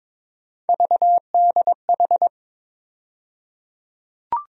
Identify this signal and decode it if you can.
Morse code, heard as a keyed tone, about 22 wpm, 700 Hz. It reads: VDH
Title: USA Call Sign Postfixes - 3 Letters 22wpm